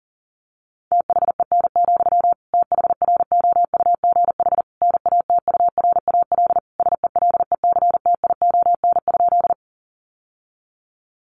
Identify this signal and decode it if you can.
Morse code, heard as a keyed tone, about 40 wpm, 700 Hz. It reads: THEN, THROUGH NATURAL SELECTION?
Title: Sentences from Top 1400 Words 40wpm